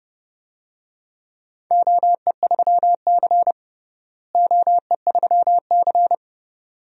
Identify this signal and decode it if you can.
Morse code, heard as a keyed tone, about 30 wpm, 700 Hz. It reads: OE3C OE3C
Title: DXCC-100 Call Signs - 4 Characters - Repeated 2x - 30wpm